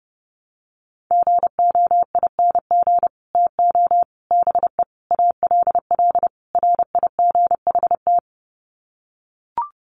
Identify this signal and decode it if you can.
Morse code, heard as a keyed tone, about 30 wpm, 700 Hz. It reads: GOING TO BE ALL RIGHT
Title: Sets of 5 Words Encore2 30wpm